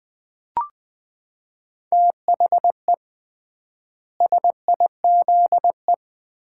Morse code, heard as a keyed tone, about 21 wpm, 700 Hz. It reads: THE SIZE
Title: 3x Word-Spacing - Sets of 4 Words - 20wpm